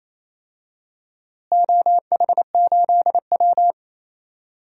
Morse code, heard as a keyed tone, about 28 wpm, 700 Hz. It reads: OH8W
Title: DXCC-100 Call Signs - 4 Characters - Repeated 2x - 28wpm